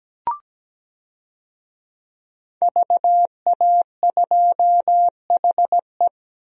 Morse code, heard as a keyed tone, about 17 wpm, 700 Hz. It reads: VA2HE
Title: DXCC-100 Call Signs - 5 Characters - Repeated 2x - 17wpm